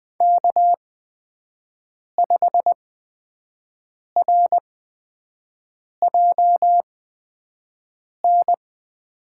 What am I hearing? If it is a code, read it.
Morse code, heard as a keyed tone, about 20 wpm, spaced as for 5 wpm, 700 Hz. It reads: K5RJN
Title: USA Call Signs - 5 Characters - 8x Character Spacing - 20wpm